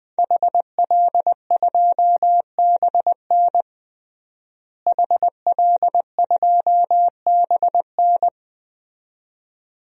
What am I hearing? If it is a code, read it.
Morse code, heard as a keyed tone, about 20 wpm, 700 Hz. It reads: HL2BN HL2BN
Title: DXCC-100 Call Signs - 5 Characters - Repeated 2x - 20wpm